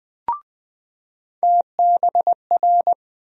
Morse code, heard as a keyed tone, about 20 wpm, 700 Hz. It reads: TBR
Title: USA Call Sign Postfixes - 3 Letters 20wpm